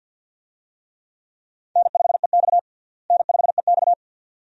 Morse code, heard as a keyed tone, about 51 wpm, 700 Hz. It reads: N5EX N5EX
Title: USA Call Signs - 4 Characters - Repeated 2x - 50wpm